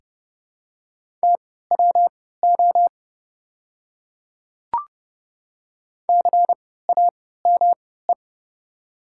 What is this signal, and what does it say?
Morse code, heard as a keyed tone, about 30 wpm, spaced as for 17 wpm, 690 Hz. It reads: TWO CAME
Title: 3x Character Spacing - Top 200 Words - 30wpm